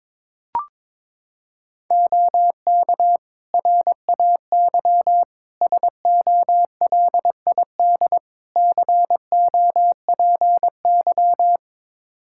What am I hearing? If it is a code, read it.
Morse code, heard as a keyed tone, about 22 wpm, 700 Hz. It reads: OK RAY SOLID COPY
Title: Practice QSO - Exchange 3 - More Info - 22wpm